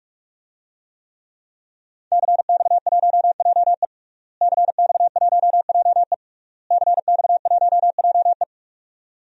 Morse code, heard as a keyed tone, about 46 wpm, 700 Hz. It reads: CX1JE CX1JE CX1JE
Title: DXCC-100 Call Signs - 5 Characters - Repeated 3x - 45wpm